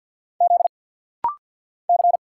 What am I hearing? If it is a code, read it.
Morse code, heard as a keyed tone, about 50 wpm, 700 Hz. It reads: Z C
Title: Single Letters - Rapid-Fire 50wpm